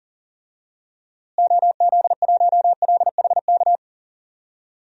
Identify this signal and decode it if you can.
Morse code, heard as a keyed tone, about 40 wpm, 700 Hz. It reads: OZ1LHK
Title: DXCC-100 Call Signs - 6 Characters - Repeated 2x - 40wpm